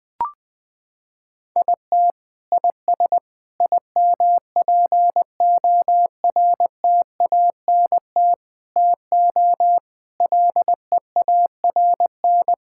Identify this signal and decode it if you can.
Morse code, heard as a keyed tone, about 20 wpm, 700 Hz. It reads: IT IS IMPORTANT TO LEARN
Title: Sentences from Top 1000 Words - Review Part 6 - 20wpm